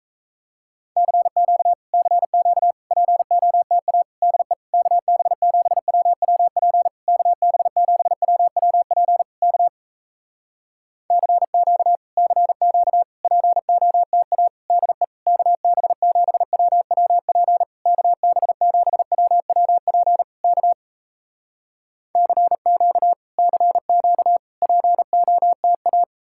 Morse code, heard as a keyed tone, about 38 wpm, 700 Hz. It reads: CQ CQ POTA DE KB7WWP KB7WWP K CQ CQ POTA DE KB7WWP KB7WWP K CQ CQ POTA